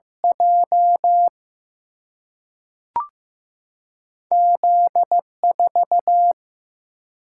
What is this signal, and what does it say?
Morse code, heard as a keyed tone, about 15 wpm, 700 Hz. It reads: J Z 4
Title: Single-Letter Single-Number 15wpm